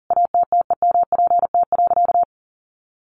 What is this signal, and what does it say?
Morse code, heard as a keyed tone, about 40 wpm, 700 Hz. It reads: ATTEMPT.
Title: Sentences from Top 1000 Words 40wpm